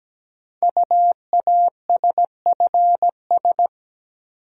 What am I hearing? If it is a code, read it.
Morse code, heard as a keyed tone, about 17 wpm, 700 Hz. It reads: UASFS
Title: Code Groups - 5 Letters - 17wpm